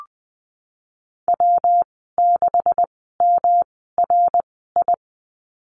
Morse code, heard as a keyed tone, about 20 wpm, spaced as for 15 wpm, 700 Hz. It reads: W6MRI
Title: 154 Lesson 25g - Letter p - Call Signs 20wpm